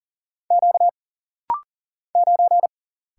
Morse code, heard as a keyed tone, about 40 wpm, 700 Hz. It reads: Q 9
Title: Single Letter-Number Alt - Rapid-Fire 40wpm